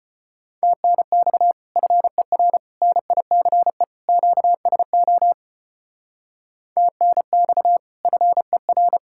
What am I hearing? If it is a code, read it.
Morse code, heard as a keyed tone, about 32 wpm, 700 Hz. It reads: TNX FER NICE QSO TNX FER